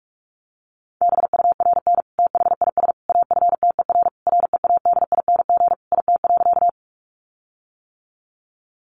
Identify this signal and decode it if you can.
Morse code, heard as a keyed tone, about 44 wpm, 700 Hz. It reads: BURN THIS AFTER READING IT.
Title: Sentences from Top 1300 Words 45wpm